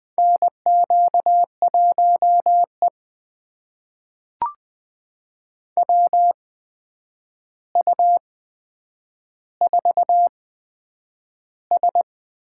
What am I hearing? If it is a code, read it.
Morse code, heard as a keyed tone, about 20 wpm, spaced as for 9 wpm, 700 Hz. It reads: NQ1E WU4S
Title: USA Call Signs - 4 Characters - 8x Character Spacing - 20wpm